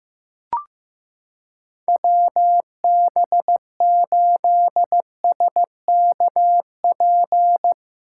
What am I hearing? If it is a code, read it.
Morse code, heard as a keyed tone, about 15 wpm, 700 Hz. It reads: WB8SKP
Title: K1USN SST QSO Practice - 15wpm